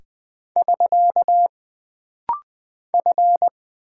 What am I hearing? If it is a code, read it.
Morse code, heard as a keyed tone, about 20 wpm, 700 Hz. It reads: <SK> F
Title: Single Character (Letter:Number:Punctuation:Prosign) - Rapid-Fire - 20wpm